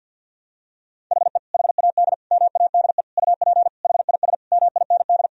Morse code, heard as a keyed tone, about 50 wpm, 700 Hz. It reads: HE HAD MADE UP HIS MIND
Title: Sentences from Top 1000 Words - Review Part 3 - 50wpm